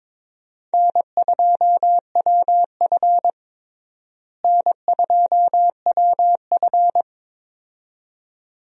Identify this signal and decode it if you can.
Morse code, heard as a keyed tone, about 22 wpm, 700 Hz. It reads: N2WF N2WF
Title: USA Call Signs - 4 Characters - Repeated 2x - 22wpm